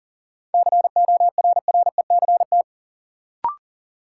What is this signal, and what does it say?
Morse code, heard as a keyed tone, about 40 wpm, 700 Hz. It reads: CORRECT
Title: Sentences from Top 700 Words - New Word Drill - 40wpm